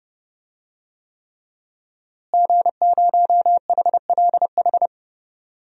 Morse code, heard as a keyed tone, about 30 wpm, 700 Hz. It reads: G0HLH
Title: DXCC-100 Call Signs - 5 Characters - Repeated 3x - 30wpm